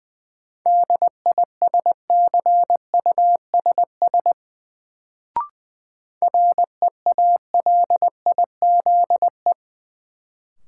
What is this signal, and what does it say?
Morse code, heard as a keyed tone, about 20 wpm, 700 Hz. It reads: DISCUSS REALIZE
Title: Sentences from Top 1000 Words - New Word Drill - 20wpm